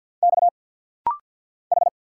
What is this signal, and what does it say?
Morse code, heard as a keyed tone, about 50 wpm, 700 Hz. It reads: X H
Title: Single Letters - Rapid-Fire 50wpm